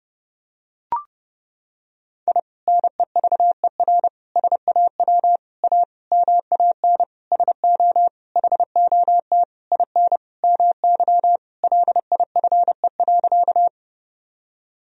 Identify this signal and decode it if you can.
Morse code, heard as a keyed tone, about 30 wpm, 700 Hz. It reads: I NEVER SAW A MAN SO HOT IN MY LIFE.
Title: Sentences from Top 1000 Words - Review Part 3 - 30wpm